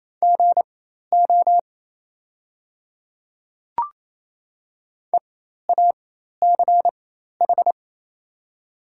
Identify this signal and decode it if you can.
Morse code, heard as a keyed tone, about 27 wpm, spaced as for 13 wpm, 700 Hz. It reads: GO EACH
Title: 4x Character Spacing - Top 100 Words - 28wpm